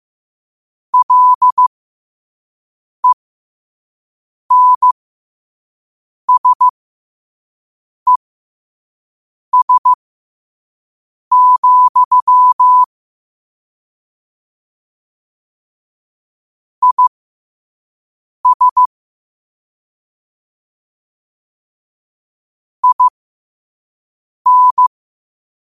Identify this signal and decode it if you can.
Morse code, heard as a keyed tone, about 15 wpm, spaced as for 5 wpm, 1000 Hz. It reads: LENSES, IS IN